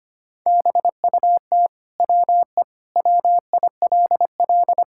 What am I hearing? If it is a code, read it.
Morse code, heard as a keyed tone, about 25 wpm, 700 Hz. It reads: BUT WE WILL
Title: Sentences from Top 1000 Words - Review Part 6 - 25wpm